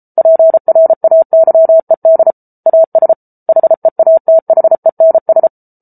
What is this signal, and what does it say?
Morse code, heard as a keyed tone, about 34 wpm, 650 Hz. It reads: PRAYED AS HEATHENS